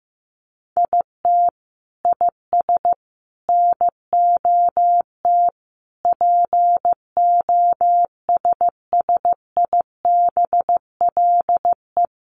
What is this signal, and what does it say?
Morse code, heard as a keyed tone, about 15 wpm, 700 Hz. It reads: IT IS NOT POSSIBLE
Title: Sentences from Top 1200 Words 15wpm